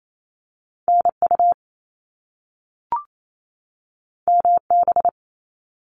Morse code, heard as a keyed tone, about 28 wpm, 700 Hz. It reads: NU MB
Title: Canadian Provinces and Territories 28wpm